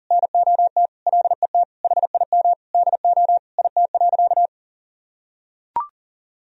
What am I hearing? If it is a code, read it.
Morse code, heard as a keyed tone, about 40 wpm, 700 Hz. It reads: NOT LET HIM DO IT.
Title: Sentences from Top 1000 Words - Review Part 3 - 40wpm